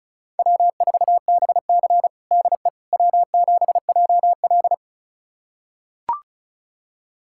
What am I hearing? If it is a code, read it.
Morse code, heard as a keyed tone, about 36 wpm, 700 Hz. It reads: W4BC DE W7JL
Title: USA Callsign DE Callsign - 4 Characters - 35wpm